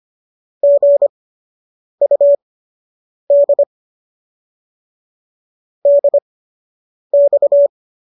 Morse code, heard as a keyed tone, about 25 wpm, spaced as for 8 wpm, 570 Hz. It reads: GUD DX